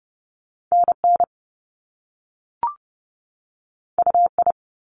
Morse code, heard as a keyed tone, about 30 wpm, 700 Hz. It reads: NN UI
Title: Top Two Letter Combinations in Words 30wpm